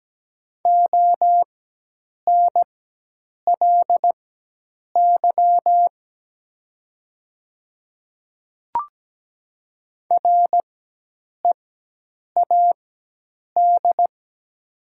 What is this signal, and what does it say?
Morse code, heard as a keyed tone, about 17 wpm, spaced as for 8 wpm, 700 Hz. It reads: ONLY READ
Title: 4x Character Spacing - Top 300 Words - 17wpm